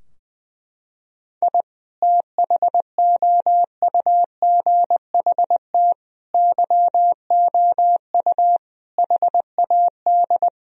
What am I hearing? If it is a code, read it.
Morse code, heard as a keyed tone, about 20 wpm, 700 Hz. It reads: I THOUGHT YOU HAD
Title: Sentences from Top 1000 Words - Review Part 3 - 20wpm